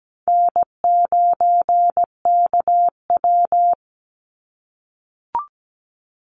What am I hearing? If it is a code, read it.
Morse code, heard as a keyed tone, about 17 wpm, 700 Hz. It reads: N9KW
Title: NA SOTA Chaser Callsigns 17wpm